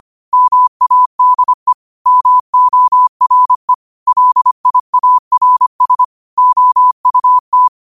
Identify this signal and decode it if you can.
Morse code, heard as a keyed tone, about 25 wpm, 1000 Hz. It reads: MADE MORE LIARS OUT